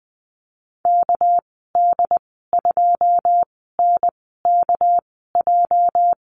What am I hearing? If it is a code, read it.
Morse code, heard as a keyed tone, about 20 wpm, spaced as for 15 wpm, 700 Hz. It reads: KD2NKJ
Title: 238 Lesson 38g - Letter q - Call Signs 20wpm